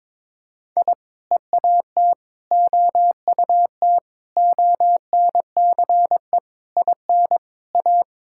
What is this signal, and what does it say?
Morse code, heard as a keyed tone, about 22 wpm, 700 Hz. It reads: I EAT OUT ONCE IN A
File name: practice / 22wpm / Sentences with 4-Word Phrases - 22wpm